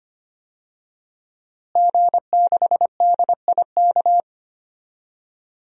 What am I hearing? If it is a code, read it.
Morse code, heard as a keyed tone, about 25 wpm, 700 Hz. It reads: G6DIK
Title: DXCC-100 Call Signs - 5 Characters - Repeated 3x - 25wpm